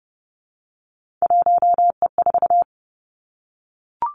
Morse code, heard as a keyed tone, about 30 wpm, 700 Hz. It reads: 1E4
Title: Three Challenging Letter-Number Combinations 30wpm